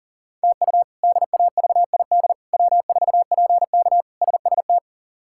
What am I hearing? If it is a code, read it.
Morse code, heard as a keyed tone, about 40 wpm, 700 Hz. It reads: TU DAVID W4PK SST
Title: K1USN SST QSO Practice - 40wpm